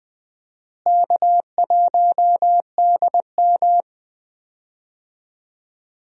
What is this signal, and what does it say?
Morse code, heard as a keyed tone, about 20 wpm, 700 Hz. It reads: K1DM
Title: USA Call Signs - 4 Characters - Repeated 2x - 20wpm